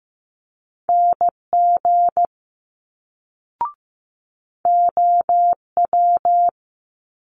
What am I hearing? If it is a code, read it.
Morse code, heard as a keyed tone, about 15 wpm, 700 Hz. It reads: NG OW
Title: Multi-letter Phonemes Quick Fire 15wpm